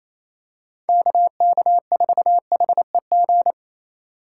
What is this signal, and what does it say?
Morse code, heard as a keyed tone, about 28 wpm, 700 Hz. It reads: KK4HEG
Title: K1USN SST QSO Practice - 28wpm